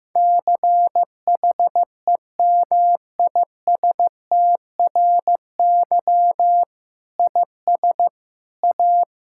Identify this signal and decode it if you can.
Morse code, heard as a keyed tone, about 15 wpm, 700 Hz. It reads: CHEMISTRY IS A
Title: Sentences from Top 1000 Words - Review Part 6 - 15wpm